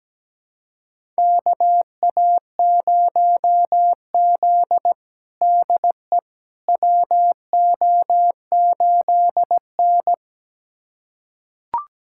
Morse code, heard as a keyed tone, about 17 wpm, 700 Hz. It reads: KA0Z DE WO8N
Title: USA Callsign DE Callsign - 4 Characters - 17wpm